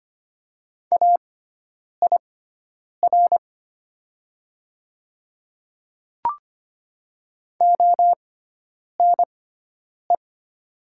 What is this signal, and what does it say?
Morse code, heard as a keyed tone, about 25 wpm, spaced as for 9 wpm, 700 Hz. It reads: AIR ONE
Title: 6x Character Spacing - Top 200 Words - 25wpm